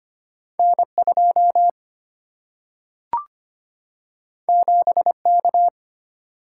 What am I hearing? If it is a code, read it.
Morse code, heard as a keyed tone, about 25 wpm, 700 Hz. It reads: N 2 7 K
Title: Single-Letter Single-Number 25wpm